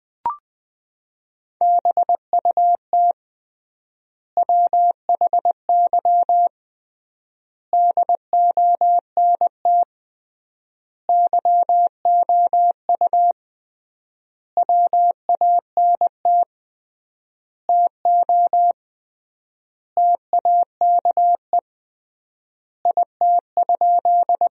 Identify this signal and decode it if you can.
Morse code, heard as a keyed tone, about 20 wpm, 700 Hz. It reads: BUT WHY DONT YOU WANT TO TAKE IT?
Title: 3x Word-Spacing - Sentences from Top 300 Words - 20wpm